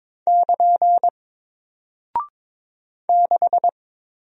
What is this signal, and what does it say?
Morse code, heard as a keyed tone, about 22 wpm, 700 Hz. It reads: <KN> 6
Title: Single Character (Letter:Number:Punctuation:Prosign) Alt 22wpm